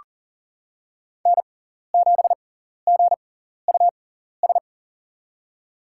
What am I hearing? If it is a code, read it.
Morse code, heard as a keyed tone, about 39 wpm, spaced as for 14 wpm, 700 Hz. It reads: N7GUS
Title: USA Call Signs - 5 Characters - 6x Character Spacing - 40wpm